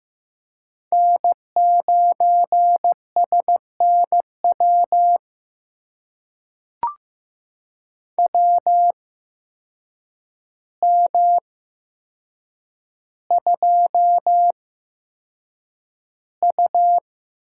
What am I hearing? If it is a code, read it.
Morse code, heard as a keyed tone, about 15 wpm, spaced as for 7 wpm, 700 Hz. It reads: N9SNW WM2U
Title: USA Call Signs - 8x Character Spacing - 15wpm